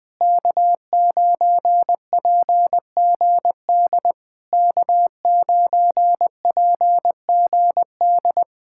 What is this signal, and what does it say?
Morse code, heard as a keyed tone, about 20 wpm, 700 Hz. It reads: K9PGD K9PGD
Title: Speed-Racing Common QSO Phrases - 17wpm